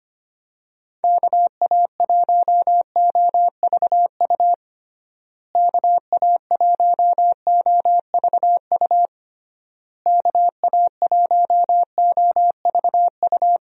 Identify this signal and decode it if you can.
Morse code, heard as a keyed tone, about 25 wpm, 700 Hz. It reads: KA1OVU KA1OVU KA1OVU
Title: USA Call Signs - 6 Characters - Repeated 3x - 25wpm